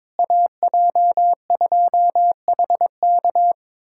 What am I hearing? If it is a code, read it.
Morse code, heard as a keyed tone, about 21 wpm, 700 Hz. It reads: AJ2HK
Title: USA Callsign DE Callsign - 5 Characters - 22wpm